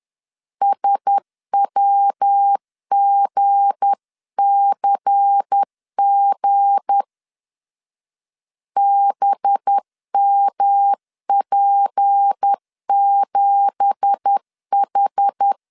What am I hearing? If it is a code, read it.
Morse code, heard as a keyed tone, about 10 wpm, 800 Hz. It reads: SWGCG BMP7H
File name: LEZIONI / 10 SIGNS / 10A SLASH 10wpm 2808